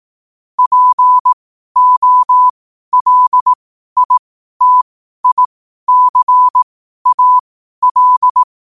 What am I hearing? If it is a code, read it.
Morse code, heard as a keyed tone, about 18 wpm, spaced as for 12 wpm, 1000 Hz. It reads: POLITICAL
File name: qotdfiles / 12wpm